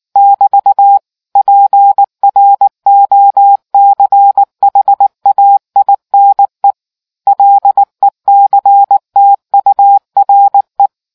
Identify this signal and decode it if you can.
Morse code, heard as a keyed tone, about 19 wpm, 800 Hz. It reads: <BT> PROCHAINE LECTURE